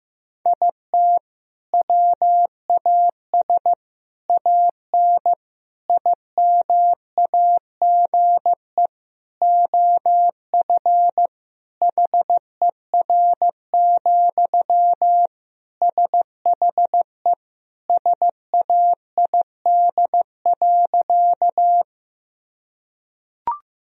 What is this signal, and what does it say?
Morse code, heard as a keyed tone, about 15 wpm, 700 Hz. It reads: IT WAS AN IMAGE OF HER, SHE SAID.
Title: Sentences from Top 1000 Words - Review Part 4 - 15wpm